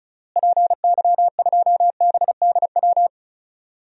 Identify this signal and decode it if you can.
Morse code, heard as a keyed tone, about 36 wpm, 700 Hz. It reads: PY2BDW
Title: DXCC-100 Call Signs - 6 Characters - Repeated 3x - 35wpm